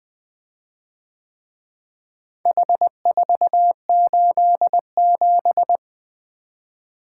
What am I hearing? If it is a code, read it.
Morse code, heard as a keyed tone, about 22 wpm, 700 Hz. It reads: H487
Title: Speed-Racing - Four Letter-Number Combinations - 20wpm